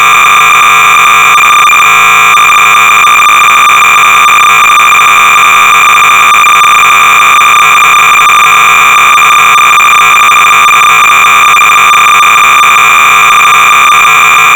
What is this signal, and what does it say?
Cassette data (Kansas City type).
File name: sorcerer / 300baud / Reversi-(1979)(Terry and Kathleen Beyer)-BASIC-Sorcerer-300baud